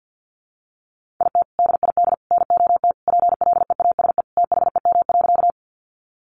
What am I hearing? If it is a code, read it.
Morse code, heard as a keyed tone, about 50 wpm, 700 Hz. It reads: IT DID NOT PLEASE THEM.
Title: Sentences from Top 1500 Words 50wpm